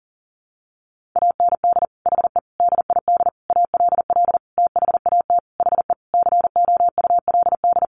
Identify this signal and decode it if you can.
Morse code, heard as a keyed tone, about 40 wpm, 700 Hz. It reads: AND HE DID ALL THAT HE COULD
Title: Sentences from Top 1500 Words 40wpm